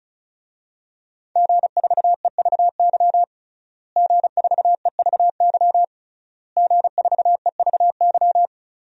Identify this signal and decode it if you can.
Morse code, heard as a keyed tone, about 36 wpm, 700 Hz. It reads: G4EVY G4EVY G4EVY
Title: DXCC-100 Call Signs - 5 Characters - Repeated 3x - 35wpm